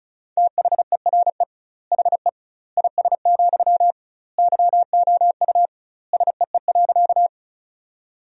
Morse code, heard as a keyed tone, about 34 wpm, 700 Hz. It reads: THERE HE IS, YOU SEE.
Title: 2x Word-Spacing - Sentences from Top 100 Words - 35wpm